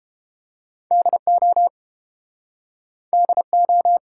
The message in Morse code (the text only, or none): DO DO